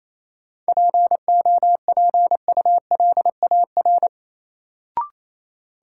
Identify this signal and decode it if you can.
Morse code, heard as a keyed tone, about 28 wpm, 700 Hz. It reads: POPULAR
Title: Sentences from Top 1000 Words - Review Part 3 - New Word Drill - 28wpm